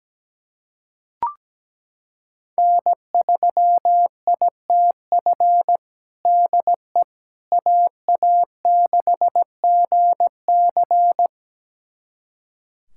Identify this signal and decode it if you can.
Morse code, heard as a keyed tone, about 17 wpm, 700 Hz. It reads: N3ITF DE AA6GC
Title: USA Callsign DE Callsign - 5 Characters - 17wpm